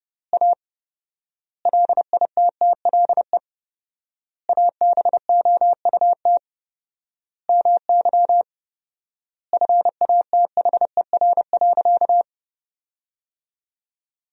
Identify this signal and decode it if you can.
Morse code, heard as a keyed tone, about 30 wpm, 700 Hz. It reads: A LITTLE ABOUT MY FATHER.
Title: 4x Word-Spacing - Sentences from Top 400 Words - 30wpm